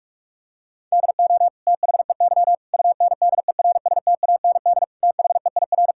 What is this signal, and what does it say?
Morse code, heard as a keyed tone, about 46 wpm, 700 Hz. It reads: DO THEY UNDERSTAND THEIR
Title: Sentences from Top 1000 Words - Review Part 6 - 45wpm